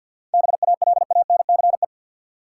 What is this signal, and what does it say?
Morse code, heard as a keyed tone, about 47 wpm, 700 Hz. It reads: BALANCE
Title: Sentences from Top 900 Words - New Word Drill - 50wpm